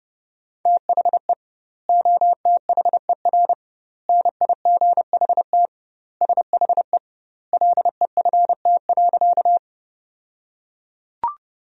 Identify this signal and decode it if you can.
Morse code, heard as a keyed tone, about 30 wpm, 700 Hz. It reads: THE OTHER NIGHT SHE LEFT.
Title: 2x Word-Spacing - Sentences from Top 300 Words - 30wpm